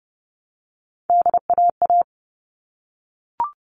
Morse code, heard as a keyed tone, about 30 wpm, 700 Hz. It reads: DAA
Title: All Cut Numbers - 3-Digit 30wpm